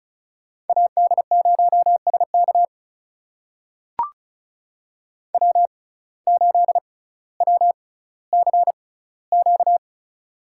Morse code, heard as a keyed tone, about 36 wpm, spaced as for 18 wpm, 700 Hz. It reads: AD0SK W8WCQ
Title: USA Call Signs - 5 Characters - 6x Character Spacing - 35wpm